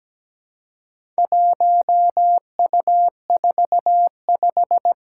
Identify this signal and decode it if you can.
Morse code, heard as a keyed tone, about 17 wpm, 700 Hz. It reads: 1U45
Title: Four Challenging Letter-Number Combinations 17wpm